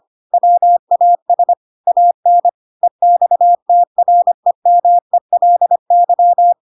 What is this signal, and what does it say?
Morse code, heard as a keyed tone, about 25 wpm, 700 Hz. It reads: WAS AN EXTREMELY
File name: Pinocchio / Pinocchio ch3 25 25 ik1hge